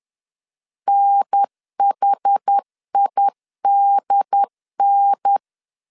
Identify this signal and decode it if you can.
Morse code, heard as a keyed tone, about 10 wpm, 800 Hz. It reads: NHIDN
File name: CW / DET / 1 3 1500C S B 10wpm 3335